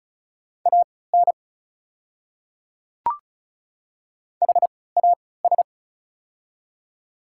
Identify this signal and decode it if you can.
Morse code, heard as a keyed tone, about 34 wpm, spaced as for 20 wpm, 700 Hz. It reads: AN HAS
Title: 3x Character Spacing - Top 200 Words - 35wpm